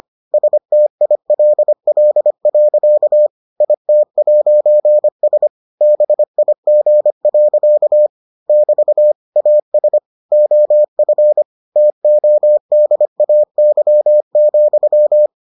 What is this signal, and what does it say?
Morse code, heard as a keyed tone, about 25 wpm, 600 Hz. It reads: STILL. IT'S BIG. <BT> AS OF TODAY,